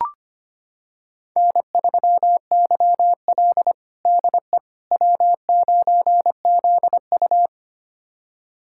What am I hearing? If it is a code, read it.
Morse code, heard as a keyed tone, about 25 wpm, 700 Hz. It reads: N3YL DE W9ZU
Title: USA Callsign DE Callsign - 4 Characters - 25wpm